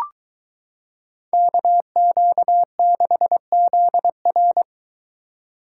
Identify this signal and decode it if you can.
Morse code, heard as a keyed tone, about 22 wpm, 700 Hz. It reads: KQ6ZR